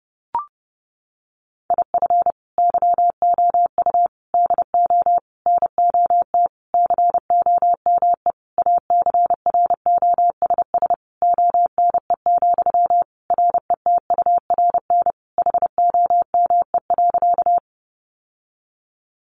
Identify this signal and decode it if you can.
Morse code, heard as a keyed tone, about 30 wpm, 700 Hz. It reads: IF YOU DO NOT COME ACROSS ONE, RETURN HOME.
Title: Sentences from Top 800 Words 30wpm